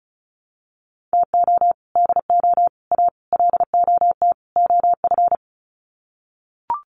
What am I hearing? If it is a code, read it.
Morse code, heard as a keyed tone, about 34 wpm, 700 Hz. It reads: TO DO A LOT OF
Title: Sets of 5 Words Encore 35wpm